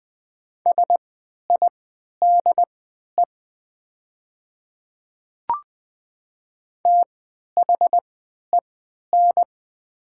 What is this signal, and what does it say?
Morse code, heard as a keyed tone, about 20 wpm, spaced as for 11 wpm, 700 Hz. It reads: SIDE THEN
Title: 3x Character Spacing - Top 100 Words - 20wpm